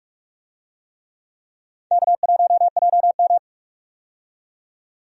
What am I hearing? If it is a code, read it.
Morse code, heard as a keyed tone, about 44 wpm, 700 Hz. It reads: K1JM
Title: USA Call Signs - 4 Characters - 45wpm